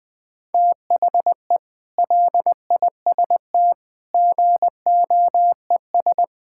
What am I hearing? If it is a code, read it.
Morse code, heard as a keyed tone, about 21 wpm, 700 Hz. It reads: THE LIST GOES
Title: Sets of 8 Words - 20wpm